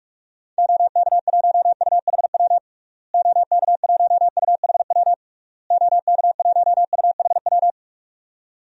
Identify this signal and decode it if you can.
Morse code, heard as a keyed tone, about 46 wpm, 700 Hz. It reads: OK1UHW OK1UHW OK1UHW
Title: DXCC-100 Call Signs - 6 Characters - Repeated 3x - 45wpm